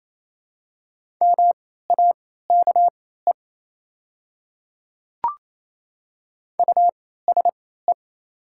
Morse code, heard as a keyed tone, about 27 wpm, spaced as for 16 wpm, 700 Hz. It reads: MAKE USE
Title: 3x Character Spacing - Top 200 Words - 28wpm